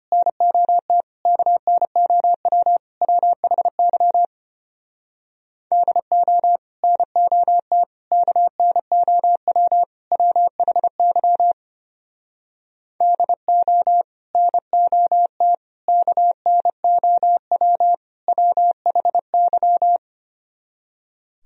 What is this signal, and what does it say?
Morse code, heard as a keyed tone, about 33 wpm, 700 Hz. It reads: NOT KNOW WHY DO NOT KNOW WHY DO NOT KNOW WHY